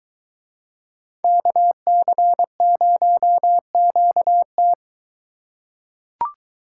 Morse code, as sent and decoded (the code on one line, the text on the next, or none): -.- -.-. ----- --.- -
KC0QT